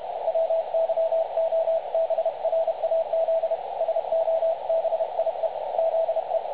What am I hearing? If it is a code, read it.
Morse code, heard as a keyed tone, about 30 wpm, 650 Hz, with radio noise. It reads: WYOBRAZ SOBIE ZE
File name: dlazaawansowanych / nagraniazpasma / websdr recording 3565.0kHz